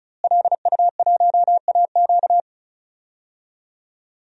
Morse code, heard as a keyed tone, about 36 wpm, 700 Hz. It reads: LU1AQ
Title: DXCC-100 Call Signs - 5 Characters - Repeated 2x - 35wpm